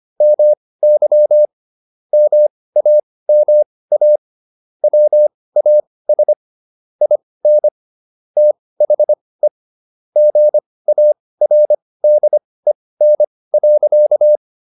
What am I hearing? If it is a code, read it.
Morse code, heard as a keyed tone, about 25 wpm, spaced as for 18 wpm, 600 Hz. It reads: MY MAMA WAS IN THE GARDEN.